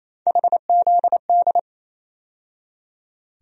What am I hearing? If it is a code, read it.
Morse code, heard as a keyed tone, about 28 wpm, 700 Hz. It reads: HZD
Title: USA Call Sign Postfixes - 3 Letters 28wpm